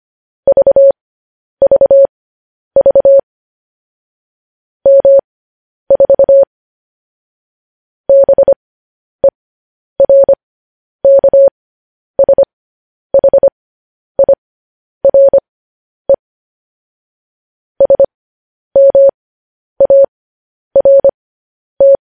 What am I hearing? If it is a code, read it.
Morse code, heard as a keyed tone, about 25 wpm, spaced as for 10 wpm, 550 Hz. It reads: VVV M4 BERKSHIRE SMART
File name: podcast / News headlines-1x-25x10